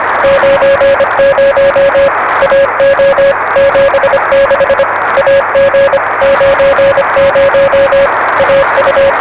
Morse code, heard as a keyed tone, about 25 wpm, 560 Hz, with noise. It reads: 90 AO 76 AG 90 AU
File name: RS7 Beacon 16 Feb 1985